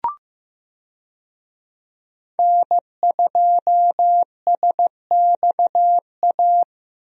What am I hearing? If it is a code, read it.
Morse code, heard as a keyed tone, about 15 wpm, 700 Hz. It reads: N2SXA
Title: USA Call Signs - 5 Characters - Repeated 2x - 15wpm